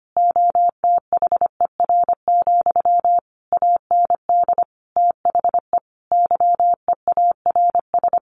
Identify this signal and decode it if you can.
Morse code, heard as a keyed tone, about 25 wpm, 700 Hz. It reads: OTHER, AND THE YEARS
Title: A Sound Of Thunder 25wpm